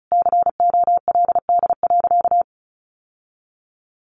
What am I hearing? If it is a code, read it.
Morse code, heard as a keyed tone, about 34 wpm, 700 Hz. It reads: COLD.
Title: Sentences from Top 1200 Words 35wpm